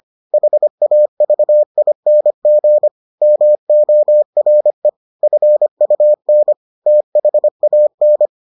Morse code, heard as a keyed tone, about 25 wpm, 600 Hz. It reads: HAVING MORE FUN THAN